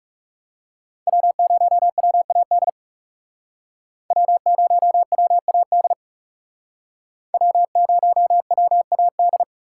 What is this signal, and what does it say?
Morse code, heard as a keyed tone, about 40 wpm, 700 Hz. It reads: W0WAD W0WAD W0WAD